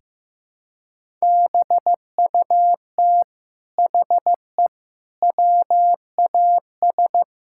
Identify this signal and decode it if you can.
Morse code, heard as a keyed tone, about 15 wpm, 700 Hz. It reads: BUT HE WAS
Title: Sentences from Top 1000 Words - Review Part 6 - 15wpm